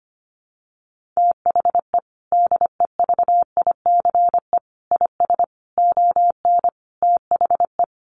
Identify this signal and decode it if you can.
Morse code, heard as a keyed tone, about 25 wpm, 700 Hz. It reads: THE DEVICE IS ON THE
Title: Sentences from Top 1500 Words 25wpm